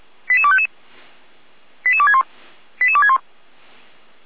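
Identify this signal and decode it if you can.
Five-tone selective call (ZVEI), heard as quick runs of five tones, tones of about 70 ms. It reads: 88255 88261 88261
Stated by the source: ZVEI five-tone call received by SDR, from feuerwehr-orlamuende fire brigade server.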